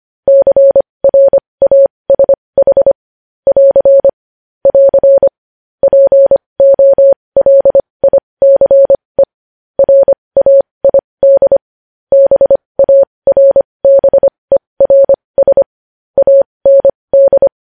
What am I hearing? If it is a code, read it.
Morse code, heard as a keyed tone, about 25 wpm, spaced as for 20 wpm, 550 Hz. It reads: CRASH <AR> <AR> POLICE RAID BARBERS AND
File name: podcast / News headlines-1x-25x20